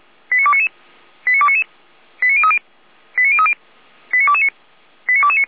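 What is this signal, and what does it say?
Five-tone selective call (ZVEI), heard as quick runs of five tones, tones of about 70 ms. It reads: 89299 89299 89030 89030 89229 89229
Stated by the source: ZVEI five-tone selective call received on SDR, Feuerwehr Orlamünde.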